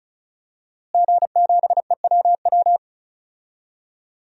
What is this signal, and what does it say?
Morse code, heard as a keyed tone, about 36 wpm, 700 Hz. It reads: G7EWW
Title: DXCC-100 Call Signs - 5 Characters - Repeated 2x - 35wpm